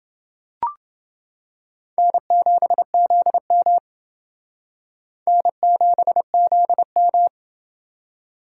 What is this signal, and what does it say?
Morse code, heard as a keyed tone, about 28 wpm, 700 Hz. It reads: N7ZM N7ZM